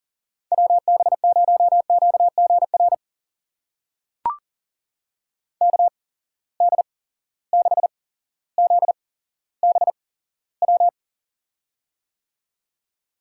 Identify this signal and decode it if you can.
Morse code, heard as a keyed tone, about 40 wpm, spaced as for 17 wpm, 700 Hz. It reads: WB0QGR KD6ZBW
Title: USA Call Signs - 6 Characters - 8x Character Spacing - 40wpm